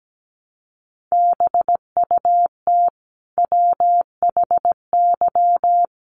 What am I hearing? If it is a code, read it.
Morse code, heard as a keyed tone, about 17 wpm, 700 Hz. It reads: BUT WHY
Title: Sentences from Top 400 Words 17wpm